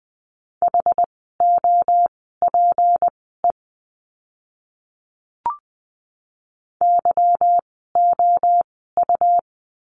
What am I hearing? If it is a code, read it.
Morse code, heard as a keyed tone, about 20 wpm, spaced as for 14 wpm, 700 Hz. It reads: HOPE YOU
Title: 153 Lesson 25f - Letter p - Words 20wpm